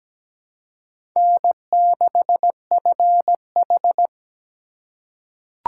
Morse code, heard as a keyed tone, about 17 wpm, 700 Hz. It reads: N6FH
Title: USA Call Signs - 4 Characters - 2x Character Spacing - 17wpm